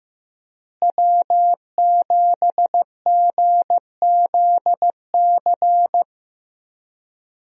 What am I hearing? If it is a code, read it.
Morse code, heard as a keyed tone, about 16 wpm, 700 Hz. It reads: W7GZC
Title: USA Call Signs - 5 Characters - Repeated 2x - 15wpm